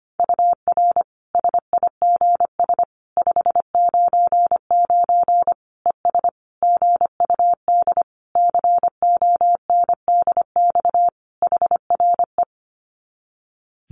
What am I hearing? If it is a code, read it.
Morse code, heard as a keyed tone, about 25 wpm, 700 Hz. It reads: UR SIGS 599 ES GUD CONDX HRE
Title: Common QSO Phrases 25wpm